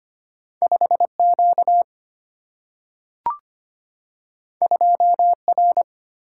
Morse code, heard as a keyed tone, about 25 wpm, 700 Hz. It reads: 5Q 2R
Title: Single-Letter Single-Number 25wpm